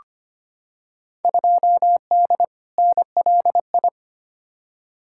Morse code, heard as a keyed tone, about 25 wpm, 700 Hz. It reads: 2D NLI
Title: Field Day Exchanges Encore - 25wpm